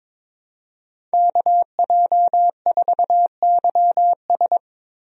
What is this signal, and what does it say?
Morse code, heard as a keyed tone, about 22 wpm, 700 Hz. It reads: KJ4YS
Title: USA Call Signs - 5 Characters - Repeated 3x - 22wpm